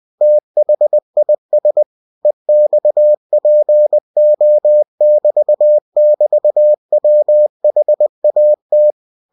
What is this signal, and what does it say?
Morse code, heard as a keyed tone, about 20 wpm, 600 Hz. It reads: THIS EXPO<BT><BT>WHAT
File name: cw-books / winnie-the-pooh / winnie-20wpm-page-0079